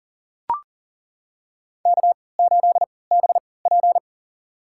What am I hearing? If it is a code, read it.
Morse code, heard as a keyed tone, about 40 wpm, spaced as for 23 wpm, 700 Hz. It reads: K8BP
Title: USA Call Signs - 3x Character Spacing - 40wpm